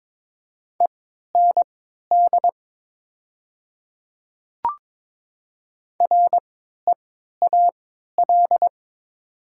Morse code, heard as a keyed tone, about 23 wpm, spaced as for 13 wpm, 700 Hz. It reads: END REAL
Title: 3x Character Spacing - Top 300 Words - 22wpm